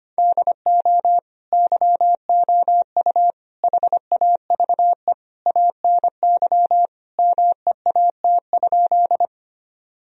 Morse code, heard as a keyed tone, about 25 wpm, 700 Hz. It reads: DO YOU HAVE ANY MEAT?
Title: Sentences from Top 1000 Words - Review Part 3 - 25wpm